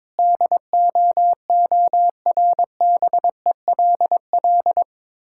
Morse code, heard as a keyed tone, about 22 wpm, 690 Hz. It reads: DOORBELL
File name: Difficult Words with Hints - 22wpm